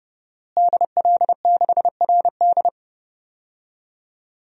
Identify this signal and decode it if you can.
Morse code, heard as a keyed tone, about 30 wpm, 700 Hz. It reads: DL6RD
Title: DXCC-100 Call Signs - 5 Characters - Repeated 2x - 30wpm